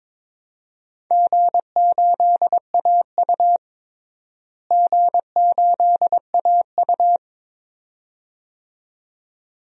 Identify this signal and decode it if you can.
Morse code, heard as a keyed tone, about 22 wpm, 700 Hz. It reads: G8AU G8AU
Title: DXCC-100 Call Signs - 4 Characters - Repeated 2x - 22wpm